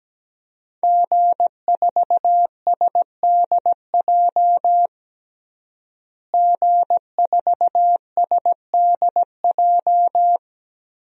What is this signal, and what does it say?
Morse code, heard as a keyed tone, about 17 wpm, 700 Hz. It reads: G4SDJ G4SDJ
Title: DXCC-100 Call Signs - 5 Characters - Repeated 2x - 17wpm